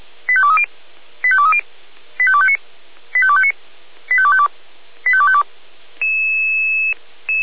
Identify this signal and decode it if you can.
Five-tone selective call (ZVEI), heard as quick runs of five tones, tones of about 70 ms. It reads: 85230 85230 85259 85259 85252 85252